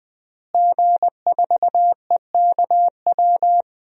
Morse code, heard as a keyed tone, about 20 wpm, 700 Hz. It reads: G4EKW
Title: DXCC-100 Call Signs - 5 Characters - Repeated 3x - 20wpm